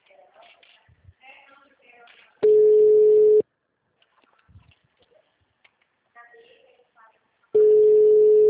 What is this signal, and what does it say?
Ringback tone (Europe, 425 Hz), with speech.